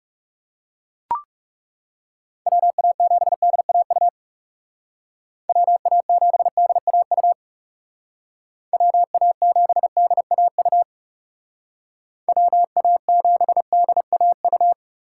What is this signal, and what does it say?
Morse code, heard as a keyed tone, about 36 wpm, 700 Hz. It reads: WA7DAU WA7DAU WA7DAU WA7DAU
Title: Speed Racing - USA 2x3 Call Signs 30wpm